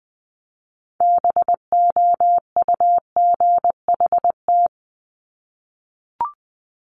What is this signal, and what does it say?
Morse code, heard as a keyed tone, about 20 wpm, 700 Hz. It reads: BOUGHT
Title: Multi-letter Phonemes Repeated 3x 20wpm